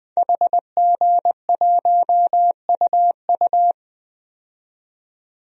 Morse code, heard as a keyed tone, about 21 wpm, 700 Hz. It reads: HG1UU
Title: DXCC-100 Call Signs - 5 Characters - 20wpm